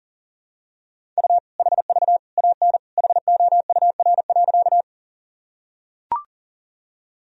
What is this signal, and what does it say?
Morse code, heard as a keyed tone, about 40 wpm, 700 Hz. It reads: U HV AN HOUR.
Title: Sentences from Top 400 Words and Abbreviations 40wpm